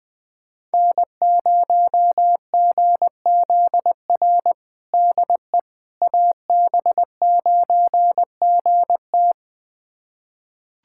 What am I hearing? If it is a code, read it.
Morse code, heard as a keyed tone, about 21 wpm, 700 Hz. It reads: N0GZR DE AB9GT
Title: USA Callsign DE Callsign - 5 Characters - 20wpm